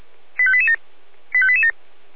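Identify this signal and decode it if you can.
Five-tone selective call (ZVEI), heard as quick runs of five tones, tones of about 80 ms. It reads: 85887 85887
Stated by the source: ZVEI five-tone selective call received on SDR.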